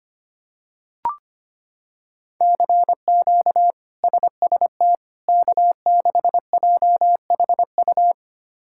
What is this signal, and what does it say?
Morse code, heard as a keyed tone, about 25 wpm, 700 Hz. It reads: CQ SST K6JHU
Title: K1USN SST QSO Practice - 25wpm